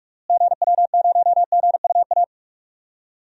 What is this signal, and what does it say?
Morse code, heard as a keyed tone, about 41 wpm, 700 Hz. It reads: GW0GUA
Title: DXCC-100 Call Signs - 6 Characters - Repeated 2x - 45wpm